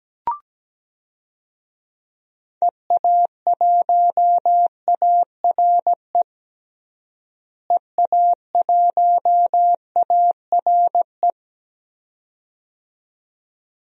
Morse code, heard as a keyed tone, about 17 wpm, 700 Hz. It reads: EA1ARE EA1ARE
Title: DXCC-100 Call Signs - 6 Characters - Repeated 2x - 17wpm